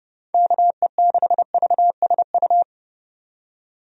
Morse code, heard as a keyed tone, about 32 wpm, 700 Hz. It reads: KE6VSU
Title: USA Call Signs - 6 Characters - 30wpm